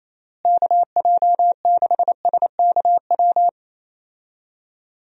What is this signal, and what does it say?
Morse code, heard as a keyed tone, about 27 wpm, 700 Hz. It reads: KJ6SKW
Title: USA Call Signs - 6 Characters - Repeated 3x - 28wpm